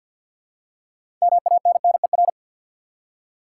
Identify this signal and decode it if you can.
Morse code, heard as a keyed tone, about 53 wpm, 700 Hz. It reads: MANNER
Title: Six Letter Words - 50wpm